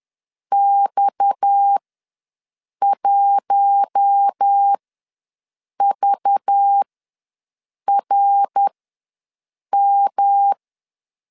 Morse code, heard as a keyed tone, about 10 wpm, spaced as for 6 wpm, 800 Hz. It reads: X1VRM
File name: LEZIONI / 7 GF9Z / 7 G ws 10wpm 3843